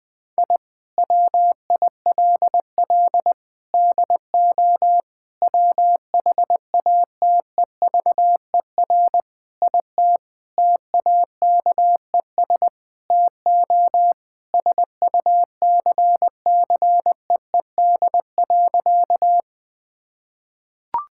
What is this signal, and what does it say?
Morse code, heard as a keyed tone, about 20 wpm, 700 Hz. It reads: I WILL DO WHATEVER IT TAKES TO SUCCEED.
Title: Sentences with 6-Word Phrases - 20wpm